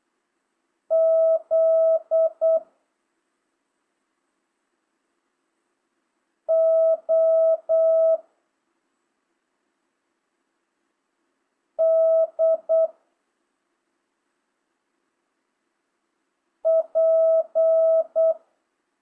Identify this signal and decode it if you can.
Morse code, heard as a keyed tone, about 8 wpm, 660 Hz. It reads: Z O D P